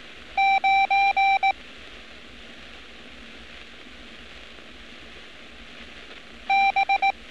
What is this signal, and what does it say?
Morse code, heard as a keyed tone, about 18 wpm, 780 Hz. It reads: 9 B
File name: training / NAVPERS 11200-11211 / NAVPERS 11206 RA RB